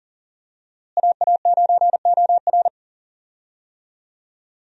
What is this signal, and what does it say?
Morse code, heard as a keyed tone, about 40 wpm, 700 Hz. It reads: AA9OR
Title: USA Call Signs - 5 Characters - Repeated 3x - 40wpm